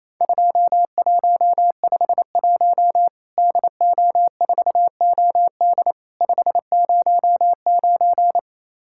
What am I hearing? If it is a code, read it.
Morse code, heard as a keyed tone, about 28 wpm, 700 Hz. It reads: 2151 DO4OD 509
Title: Worked All Europe DX Contest - QTC Exchange - 28wpm